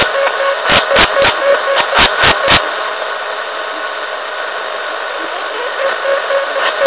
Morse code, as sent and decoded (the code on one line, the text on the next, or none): none